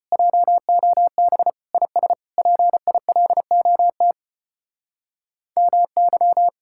Morse code, heard as a keyed tone, about 32 wpm, 700 Hz. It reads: JOB IS PILOT MY